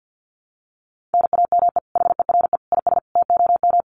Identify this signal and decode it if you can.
Morse code, heard as a keyed tone, about 50 wpm, 700 Hz. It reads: NAME HERE IS TOM
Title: Common QSO Phrases 50wpm